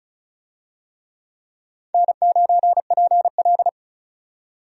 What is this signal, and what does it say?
Morse code, heard as a keyed tone, about 34 wpm, 700 Hz. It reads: N9PL
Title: USA Call Signs - 4 Characters - 35wpm